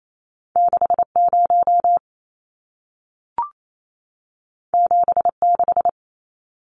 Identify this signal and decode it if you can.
Morse code, heard as a keyed tone, about 28 wpm, 700 Hz. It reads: 60 76
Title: Numbers - 2-Digit 28wpm